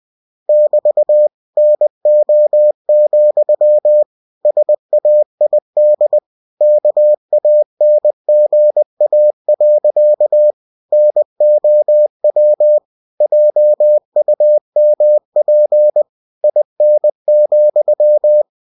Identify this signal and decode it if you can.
Morse code, heard as a keyed tone, about 20 wpm, 600 Hz. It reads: <BT> NO, SAID KANGA. NOW JUMP IN,
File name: cw-books / winnie-the-pooh / winnie-20wpm-page-0069